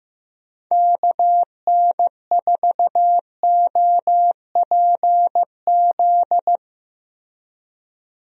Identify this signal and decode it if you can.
Morse code, heard as a keyed tone, about 15 wpm, 700 Hz. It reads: KN4OPZ
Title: USA Call Signs - 6 Characters - Repeated 2x - 15wpm